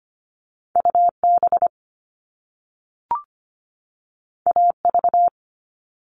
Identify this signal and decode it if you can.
Morse code, heard as a keyed tone, about 25 wpm, 700 Hz. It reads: UB AV
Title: All Cut Numbers - 2-Digit 25wpm